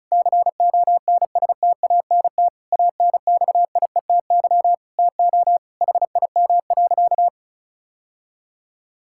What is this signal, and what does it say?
Morse code, heard as a keyed tone, about 34 wpm, 700 Hz. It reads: CONSTANT ANXIETY TO HIM.
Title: Sentences from Top 1600 Words 35wpm